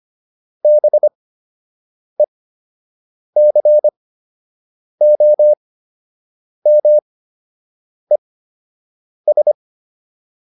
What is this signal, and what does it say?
Morse code, heard as a keyed tone, about 25 wpm, spaced as for 7 wpm, 600 Hz. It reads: BECOMES